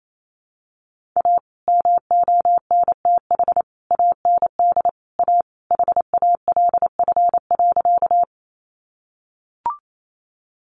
Morse code, heard as a keyed tone, about 28 wpm, 700 Hz. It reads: A MONTH AND A HALF.
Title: Sentences from Top 800 Words 28wpm